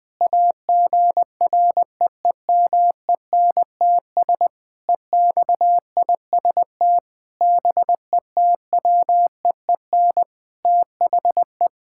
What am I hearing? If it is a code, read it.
Morse code, heard as a keyed tone, about 20 wpm, 700 Hz. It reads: AGREEMENTS EXIST BETWEEN THE
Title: Sentences with 5-Word Phrases - Spoken Beginning - 20wpm